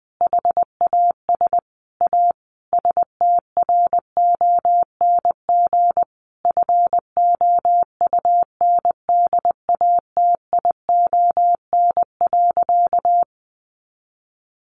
Morse code, heard as a keyed tone, about 20 wpm, 700 Hz. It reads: HAS A STRONG FOUNDATION.
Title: Sentences from Top 1200 Words 20wpm